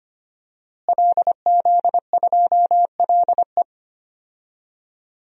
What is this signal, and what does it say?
Morse code, heard as a keyed tone, about 25 wpm, 700 Hz. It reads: LZ2LE
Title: DXCC-100 Call Signs - 5 Characters - Repeated 2x - 25wpm